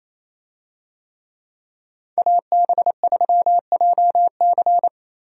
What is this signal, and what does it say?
Morse code, heard as a keyed tone, about 28 wpm, 700 Hz. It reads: AB3JC
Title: USA Call Signs - 5 Characters - Repeated 2x - 28wpm